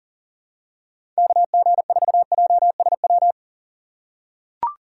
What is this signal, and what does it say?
Morse code, heard as a keyed tone, about 40 wpm, 700 Hz. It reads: KG4JSW
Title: USA Call Signs - 6 Characters - 6x Character Spacing - 40wpm